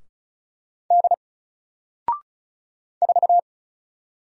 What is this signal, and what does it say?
Morse code, heard as a keyed tone, about 36 wpm, 700 Hz. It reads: D 4
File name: Single-Letter-Number-35wpm